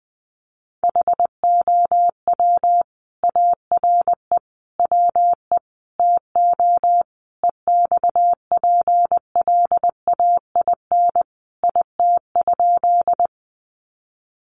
Morse code, heard as a keyed tone, about 20 wpm, 700 Hz. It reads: HOW ARE WE TO EXPLAIN IT?
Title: Sentences from Top 1100 Words 20wpm